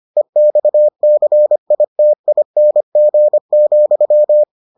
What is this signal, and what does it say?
Morse code, heard as a keyed tone, about 25 wpm, 600 Hz. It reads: EXCITING,